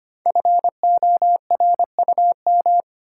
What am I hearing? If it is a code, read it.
Morse code, heard as a keyed tone, about 25 wpm, 700 Hz. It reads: FORUM
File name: 5 Letter Words - First 3 Letters Given - 25wpm